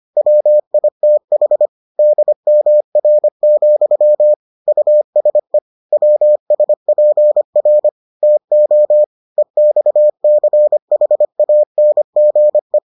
Morse code, heard as a keyed tone, about 25 wpm, 600 Hz. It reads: WITH DMR, USE WSPR TO EXCHANGE